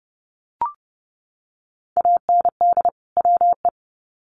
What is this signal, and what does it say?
Morse code, heard as a keyed tone, about 30 wpm, 700 Hz. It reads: AND WE
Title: Sets of 2 Words 30wpm